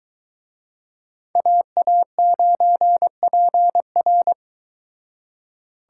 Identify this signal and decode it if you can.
Morse code, heard as a keyed tone, about 23 wpm, 700 Hz. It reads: AA9PR